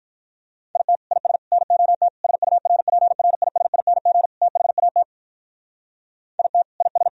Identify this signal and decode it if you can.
Morse code, heard as a keyed tone, about 44 wpm, 700 Hz. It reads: IT IS NOT SURPRISING THAT IT IS
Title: Speed-Racing - Sets of 5 Words 35wpm